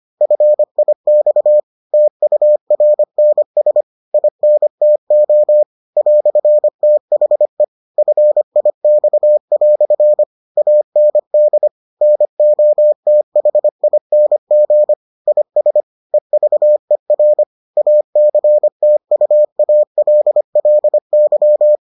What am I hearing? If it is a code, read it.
Morse code, heard as a keyed tone, about 25 wpm, 600 Hz. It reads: FIX TURNS INTO "THE FIX" AND NOTHING IS EVER ACTUALLY